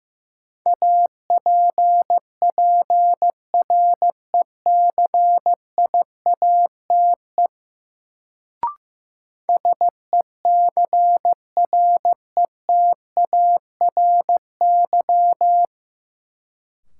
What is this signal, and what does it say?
Morse code, heard as a keyed tone, about 15 wpm, 700 Hz. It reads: APPRECIATE SECRETARY
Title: Sentences from Top 1300 Words - New Word Drill Alt - 15wpm